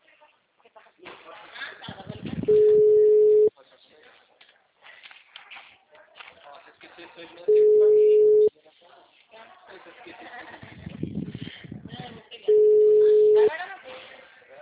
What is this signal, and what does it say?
Ringback tone (Europe, 425 Hz), with speech.